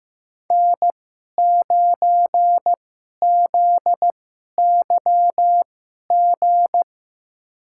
Morse code, heard as a keyed tone, about 15 wpm, spaced as for 11 wpm, 700 Hz. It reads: N9ZYG
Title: USA Call Signs - 2x Character Spacing - 15wpm